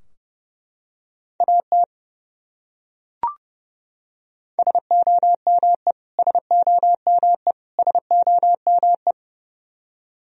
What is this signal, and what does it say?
Morse code, heard as a keyed tone, about 31 wpm, 700 Hz. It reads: AT SOME SOME SOME
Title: Top 100 Words Repeated 30wpm